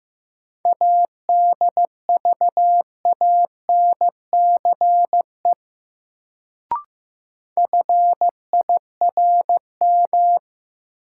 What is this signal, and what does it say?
Morse code, heard as a keyed tone, about 15 wpm, 700 Hz. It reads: ADVANCE FIRM
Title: Sentences from Top 1400 Words - New Word Drill - 15wpm